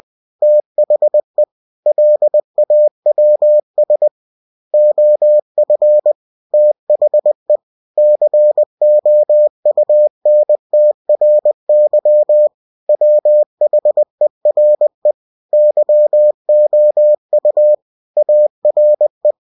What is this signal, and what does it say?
Morse code, heard as a keyed tone, about 20 wpm, 600 Hz. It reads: THE LAWS OF THE COUNTRY WHERE YOU ARE
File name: cw-books / winnie-the-pooh / winnie-20wpm-page-0115